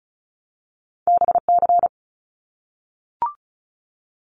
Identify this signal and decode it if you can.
Morse code, heard as a keyed tone, about 36 wpm, 700 Hz. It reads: BC
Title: Canadian Provinces and Territories 35wpm